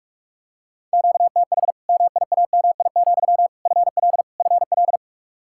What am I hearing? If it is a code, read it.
Morse code, heard as a keyed tone, about 46 wpm, 700 Hz. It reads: QTH MIAMI, FL FL
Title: Practice QSO - Exchange 1 - Beginning - 45wpm